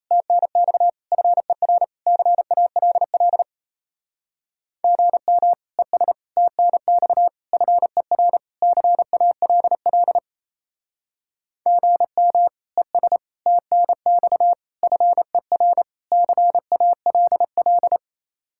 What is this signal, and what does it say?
Morse code, heard as a keyed tone, about 34 wpm, 700 Hz. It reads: TNX FER CALL GM ES TNX FER CALL GM ES TNX FER CALL
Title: Speed-Racing Common QSO Phrases Encore - 28wpm